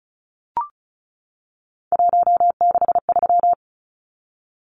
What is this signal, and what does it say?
Morse code, heard as a keyed tone, about 34 wpm, 700 Hz. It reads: 163
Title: Numbers - 3-Digit 35wpm